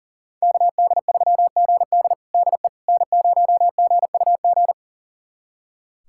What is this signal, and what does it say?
Morse code, heard as a keyed tone, about 40 wpm, 700 Hz. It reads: KD3GD DE N0GUG
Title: USA Callsign DE Callsign - 5 Characters - 40wpm